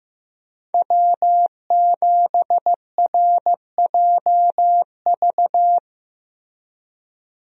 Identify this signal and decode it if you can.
Morse code, heard as a keyed tone, about 15 wpm, 700 Hz. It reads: W7RJV
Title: USA Call Signs - 5 Characters - Repeated 3x - 15wpm